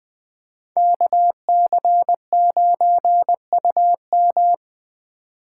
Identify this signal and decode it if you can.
Morse code, heard as a keyed tone, about 20 wpm, 700 Hz. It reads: KC9UM
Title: Speed Racing - USA 2x2 Call Signs 17wpm